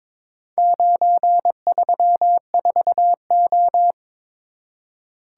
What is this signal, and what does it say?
Morse code, heard as a keyed tone, about 22 wpm, 700 Hz. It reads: 934O
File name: practice / 22wpm / Four Letter-Number Combinations - 22wpm